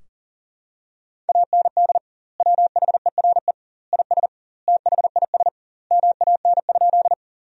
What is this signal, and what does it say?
Morse code, heard as a keyed tone, about 40 wpm, 700 Hz. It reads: AND WHERE IS THIS MAN?
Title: 2x Word-Spacing - Sentences from Top 200 Words - 40wpm